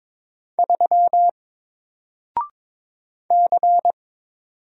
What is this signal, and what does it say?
Morse code, heard as a keyed tone, about 22 wpm, 700 Hz. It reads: 3 C
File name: Single-Letter-Number-22wpm